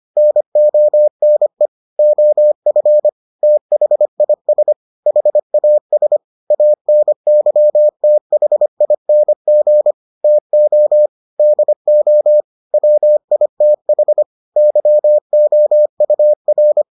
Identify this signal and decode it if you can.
Morse code, heard as a keyed tone, about 25 wpm, 600 Hz. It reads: NONE OF THIS HAS ANYTHING TO DO WITH YOUR